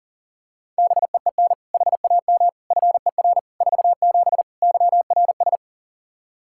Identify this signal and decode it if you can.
Morse code, heard as a keyed tone, about 40 wpm, 700 Hz. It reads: BEEN HAM FER 47 YRS
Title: Practice QSO - Exchange 3 - More Info - 40wpm